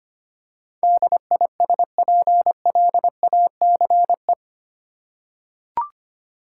Morse code, heard as a keyed tone, about 25 wpm, 700 Hz. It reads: DISPLACE
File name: 8 Letter Words - First 3 Letters Given - 25wpm